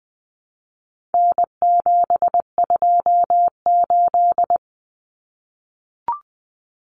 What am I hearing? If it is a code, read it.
Morse code, heard as a keyed tone, about 20 wpm, 700 Hz. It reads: N728
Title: Common Cut Numbers - 4-Digit 20wpm